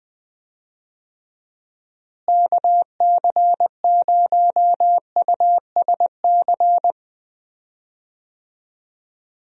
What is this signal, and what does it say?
Morse code, heard as a keyed tone, about 21 wpm, 700 Hz. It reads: KC0USC
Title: USA Call Signs - 6 Characters - Repeated 3x - 20wpm